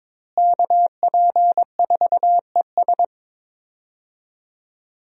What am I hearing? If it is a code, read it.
Morse code, heard as a keyed tone, about 22 wpm, 700 Hz. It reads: KP4ES
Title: USA Call Signs - 5 Characters - Repeated 3x - 22wpm